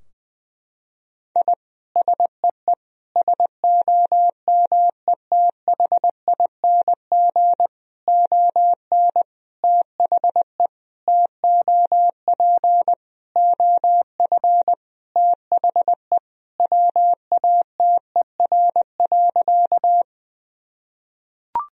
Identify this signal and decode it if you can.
Morse code, heard as a keyed tone, about 21 wpm, 700 Hz. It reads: I SEE SOMETHING ON THE TOP OF THE WATER.
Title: Sentences from Top 1000 Words - Review Part 2 - 20wpm